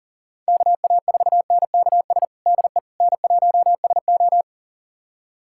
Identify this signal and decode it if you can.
Morse code, heard as a keyed tone, about 37 wpm, 700 Hz. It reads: KA4NKS DE N1SO
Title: USA Callsign DE Callsign 40wpm